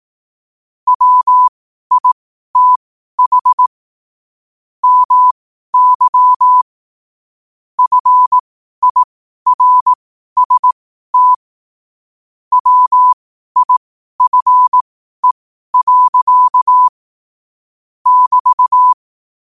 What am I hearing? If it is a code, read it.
Morse code, heard as a keyed tone, about 18 wpm, spaced as for 12 wpm, 1000 Hz. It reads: WITH MY FIRST WIFE. <BT>
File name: qotdfiles / 12wpm